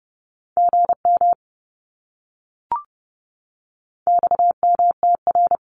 Morse code, heard as a keyed tone, about 30 wpm, 700 Hz. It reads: GM XMTR
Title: Abbreviations Repeated 3x - 30wpm